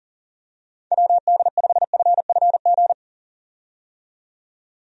Morse code, heard as a keyed tone, about 40 wpm, 700 Hz. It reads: WD5FFG
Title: USA Call Signs - 6 Characters - 40wpm